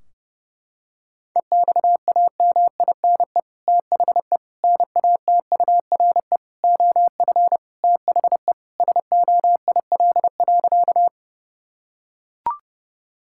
Morse code, heard as a keyed tone, about 30 wpm, 700 Hz. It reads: EXAMINE THE NATURE OF THE SOIL.
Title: Sentences from Top 1600 Words 30wpm